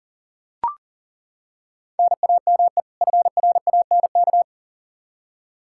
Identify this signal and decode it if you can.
Morse code, heard as a keyed tone, about 40 wpm, 700 Hz. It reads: NAME FRANK
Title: K1USN SST QSO Practice 40wpm